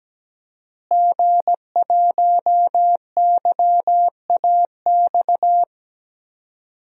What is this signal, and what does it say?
Morse code, heard as a keyed tone, about 17 wpm, 700 Hz. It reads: G1YAX
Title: DXCC-100 Call Signs - 5 Characters - Repeated 3x - 17wpm